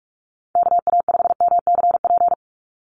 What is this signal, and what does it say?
Morse code, heard as a keyed tone, about 44 wpm, 700 Hz. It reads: KA5MCP
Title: USA Call Signs 45wpm